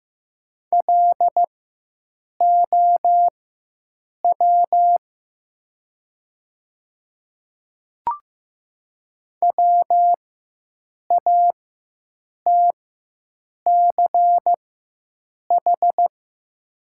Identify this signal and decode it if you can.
Morse code, heard as a keyed tone, about 15 wpm, spaced as for 7 wpm, 700 Hz. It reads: LOW WATCH
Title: 4x Character Spacing - Top 300 Words - 15wpm